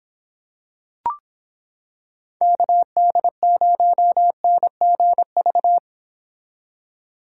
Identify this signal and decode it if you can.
Morse code, heard as a keyed tone, about 25 wpm, 700 Hz. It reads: KD0NGV